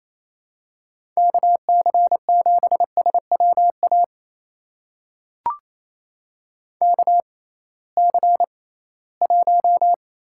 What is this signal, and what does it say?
Morse code, heard as a keyed tone, about 28 wpm, spaced as for 18 wpm, 700 Hz. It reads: KC7SWA KC1
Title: USA Call Signs - 6 Characters - 6x Character Spacing - 28wpm